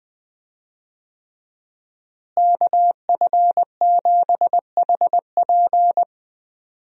Morse code, heard as a keyed tone, about 20 wpm, 700 Hz. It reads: KF7HP
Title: USA Call Signs - 5 Characters - Repeated 3x - 20wpm